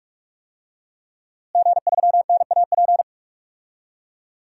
Morse code, heard as a keyed tone, about 46 wpm, 700 Hz. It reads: G3NAP
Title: DXCC-100 Call Signs - 5 Characters - Repeated 3x - 45wpm